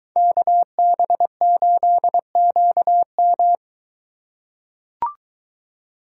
Morse code, heard as a keyed tone, about 23 wpm, 700 Hz. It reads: KB8QM